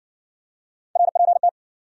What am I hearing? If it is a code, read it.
Morse code, heard as a keyed tone, about 58 wpm, 700 Hz. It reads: RPT